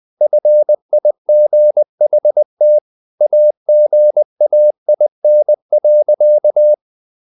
Morse code, heard as a keyed tone, about 20 wpm, 600 Hz. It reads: FIGHT AGAIN.
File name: the old man and the sea part 2 20wpm